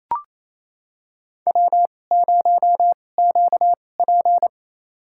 Morse code, heard as a keyed tone, about 28 wpm, spaced as for 20 wpm, 700 Hz. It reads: W0QP
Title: USA Call Signs - 4 Characters - 2x Character Spacing - 28wpm